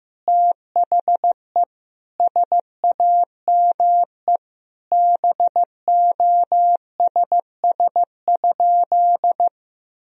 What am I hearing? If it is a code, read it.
Morse code, heard as a keyed tone, about 15 wpm, 700 Hz. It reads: THE SAME BOSS?
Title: Sentences from Top 1000 Words - Review Part 4 - 15wpm